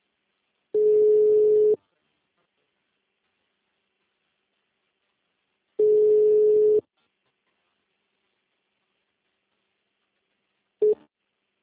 Ringback tone (Europe, 425 Hz).